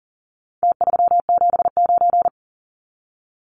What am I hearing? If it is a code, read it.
Morse code, heard as a keyed tone, about 40 wpm, 700 Hz. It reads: T379
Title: Numbers - Contest Style - 4-Digit 40wpm